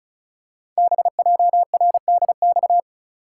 Morse code, heard as a keyed tone, about 34 wpm, 700 Hz. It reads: BJRDX
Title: Code Groups - 5 Letters - 35wpm